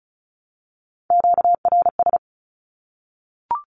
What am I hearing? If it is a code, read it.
Morse code, heard as a keyed tone, about 34 wpm, 700 Hz. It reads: QRS
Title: Q-Codes Repeated 3x 35wpm